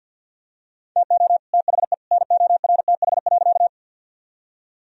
Morse code, heard as a keyed tone, about 46 wpm, 700 Hz. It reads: TO THE NORTH.
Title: Sentences from Top 1000 Words - Review Part 1 - 50wpm